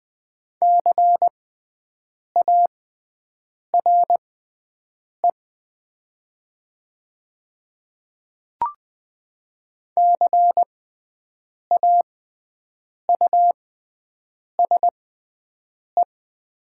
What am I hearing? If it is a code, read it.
Morse code, heard as a keyed tone, about 20 wpm, spaced as for 7 wpm, 700 Hz. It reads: CARE CAUSE
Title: 6x Character Spacing - Top 300 Words - 20wpm